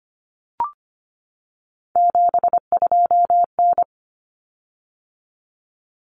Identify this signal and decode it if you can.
Morse code, heard as a keyed tone, about 25 wpm, 700 Hz. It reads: 72N
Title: Common Cut Numbers - 3-Digit 25wpm